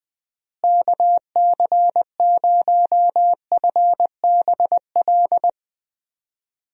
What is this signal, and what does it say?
Morse code, heard as a keyed tone, about 20 wpm, 700 Hz. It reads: KC0FBL
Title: USA Call Signs - 6 Characters - Repeated 2x - 20wpm